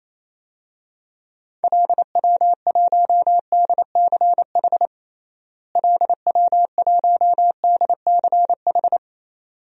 Morse code, heard as a keyed tone, about 28 wpm, 700 Hz. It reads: LW1DCH LW1DCH
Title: DXCC-100 Call Signs - 6 Characters - Repeated 2x - 28wpm